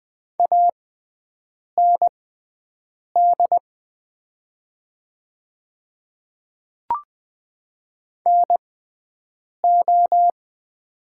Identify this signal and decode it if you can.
Morse code, heard as a keyed tone, about 20 wpm, spaced as for 7 wpm, 700 Hz. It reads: AND NO
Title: 6x Character Spacing - Top 300 Words - 20wpm